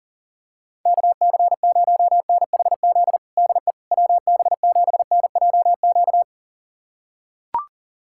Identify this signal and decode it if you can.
Morse code, heard as a keyed tone, about 40 wpm, 690 Hz. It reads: KC0NHZ DE WB7NJQ
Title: USA Callsign DE Callsign - 6 Characters - 40wpm